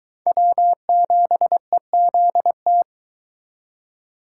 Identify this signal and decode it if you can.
Morse code, heard as a keyed tone, about 22 wpm, 700 Hz. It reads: W7EZT